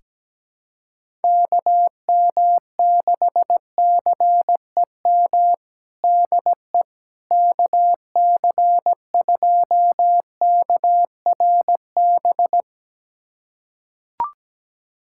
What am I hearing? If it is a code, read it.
Morse code, heard as a keyed tone, about 17 wpm, 700 Hz. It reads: KM6CEM DE KC2KRB
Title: USA Callsign DE Callsign - 6 Characters - 17wpm